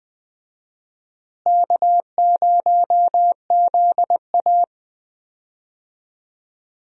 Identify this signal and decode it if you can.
Morse code, heard as a keyed tone, about 20 wpm, 700 Hz. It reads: K0ZA
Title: USA Call Signs - 4 Characters - Repeated 2x - 20wpm